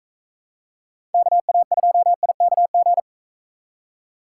Morse code, heard as a keyed tone, about 44 wpm, 700 Hz. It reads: KA2IKG